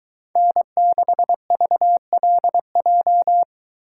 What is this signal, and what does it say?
Morse code, heard as a keyed tone, about 22 wpm, 700 Hz. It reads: N6VLJ